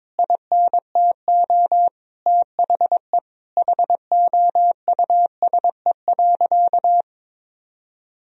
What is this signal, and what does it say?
Morse code, heard as a keyed tone, about 22 wpm, 700 Hz. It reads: INTO THE HOUSE.
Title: Sentences from Top 1000 Words - Review Part 4 - 22wpm